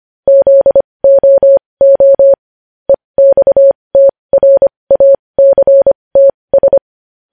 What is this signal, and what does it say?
Morse code, heard as a keyed tone, about 25 wpm, spaced as for 20 wpm, 550 Hz. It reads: ZOO EXTRACTS
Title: News Headlines at 20